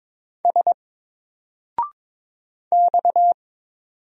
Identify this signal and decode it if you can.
Morse code, heard as a keyed tone, about 22 wpm, 700 Hz. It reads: S X
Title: Single Letter-Number Alt 22wpm